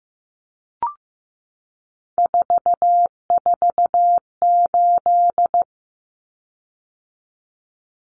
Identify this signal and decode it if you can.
Morse code, heard as a keyed tone, about 15 wpm, 700 Hz. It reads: 448
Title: Common Cut Numbers - 3-Digit 15wpm